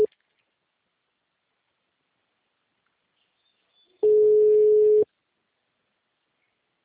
Ringback tone (Europe, 425 Hz).
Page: Itapetininga